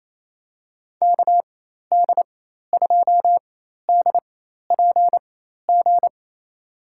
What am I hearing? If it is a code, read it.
Morse code, heard as a keyed tone, about 28 wpm, spaced as for 13 wpm, 700 Hz. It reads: KD2DPG
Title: USA Call Signs - 6 Characters - 4x Character Spacing - 28wpm